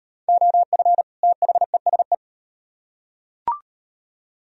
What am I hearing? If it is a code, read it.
Morse code, heard as a keyed tone, about 37 wpm, 700 Hz. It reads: OF THESE